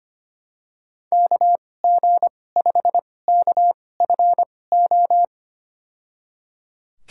Morse code, heard as a keyed tone, about 25 wpm, spaced as for 18 wpm, 700 Hz. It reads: KG5KFO
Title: USA Call Signs - 6 Characters - 2x Character Spacing - 25wpm